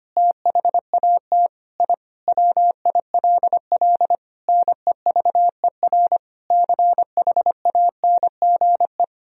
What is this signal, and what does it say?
Morse code, heard as a keyed tone, about 25 wpm, 700 Hz. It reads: THAT I WILL NEVER CHANGE
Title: Sentences from Top 1000 Words - Review Part 1 - 25wpm